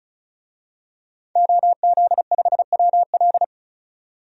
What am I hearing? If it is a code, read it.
Morse code, heard as a keyed tone, about 34 wpm, 700 Hz. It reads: OZ5WL
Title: DXCC-100 Call Signs - 5 Characters - 35wpm